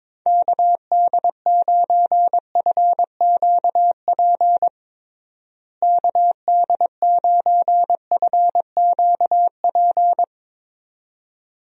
Morse code, heard as a keyed tone, about 21 wpm, 700 Hz. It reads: KD9FQP KD9FQP
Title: USA Call Signs - 6 Characters - Repeated 2x - 22wpm